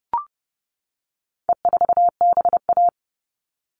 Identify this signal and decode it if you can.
Morse code, heard as a keyed tone, about 30 wpm, 700 Hz. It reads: E4BA
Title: All Cut Numbers - 4-Digit 30wpm